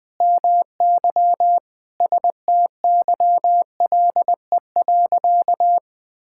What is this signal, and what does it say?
Morse code, heard as a keyed tone, about 21 wpm, 700 Hz. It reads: MY STYLE.
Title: Sentences from Top 1000 Words - Review Part 5 - 20wpm